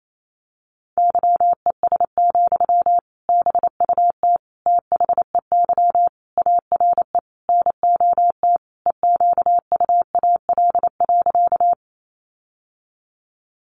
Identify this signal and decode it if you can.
Morse code, heard as a keyed tone, about 28 wpm, 700 Hz. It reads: YES, BUT THEY ARE NOT EQUAL.
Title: Sentences from Top 1400 Words 28wpm